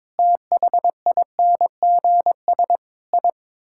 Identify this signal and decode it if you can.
Morse code, heard as a keyed tone, about 21 wpm, 700 Hz. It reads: THINGS I
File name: practice / 22wpm / Build A Phrase - 5-Word Phrases - 22wpm